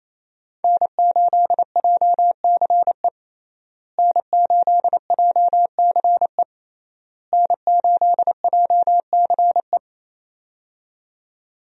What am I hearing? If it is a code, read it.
Morse code, heard as a keyed tone, about 28 wpm, 700 Hz. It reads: N8JCE N8JCE N8JCE
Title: USA Call Signs - 5 Characters - Repeated 3x - 28wpm